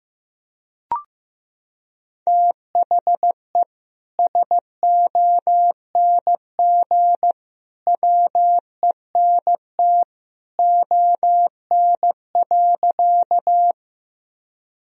Morse code, heard as a keyed tone, about 15 wpm, 700 Hz. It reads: THE SONG WENT ON.
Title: Sentences from Top 1000 Words - Review Part 2 - 15wpm